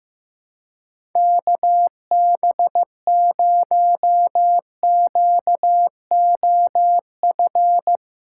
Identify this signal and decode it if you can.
Morse code, heard as a keyed tone, about 15 wpm, 700 Hz. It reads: KB0QOF
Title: USA Call Signs - 6 Characters - 2x Character Spacing - 15wpm